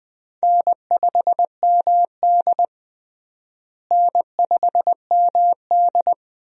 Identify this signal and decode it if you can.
Morse code, heard as a keyed tone, about 20 wpm, 700 Hz. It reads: N5MD N5MD
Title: USA Call Signs - 4 Characters - Repeated 2x - 20wpm